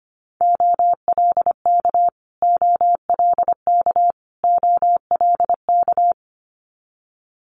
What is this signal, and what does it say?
Morse code, heard as a keyed tone, about 25 wpm, 700 Hz. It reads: OLK OLK OLK
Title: Multi-letter Phonemes Repeated 3x 25wpm